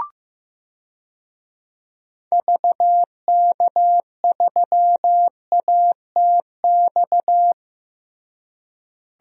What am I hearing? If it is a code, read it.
Morse code, heard as a keyed tone, about 16 wpm, 700 Hz. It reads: VK3ATX
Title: DXCC-100 Call Signs - 6 Characters - Repeated 3x - 15wpm